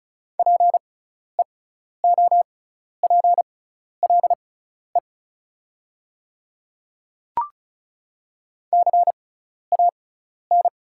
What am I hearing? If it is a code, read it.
Morse code, heard as a keyed tone, about 36 wpm, spaced as for 12 wpm, 690 Hz. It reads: PEOPLE CAN
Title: 6x Character Spacing - Top 100 Words - 35wpm